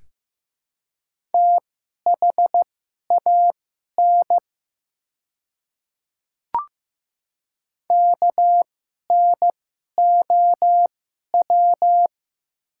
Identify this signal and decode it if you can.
Morse code, heard as a keyed tone, about 15 wpm, spaced as for 11 wpm, 700 Hz. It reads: THAN KNOW
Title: 2x Character Spacing - Top 100 Words - 15wpm